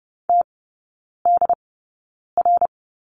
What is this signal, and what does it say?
Morse code, heard as a keyed tone, about 30 wpm, spaced as for 9 wpm, 690 Hz. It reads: TDR
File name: Lesson 17e - Letter u - Three Characters 30wpm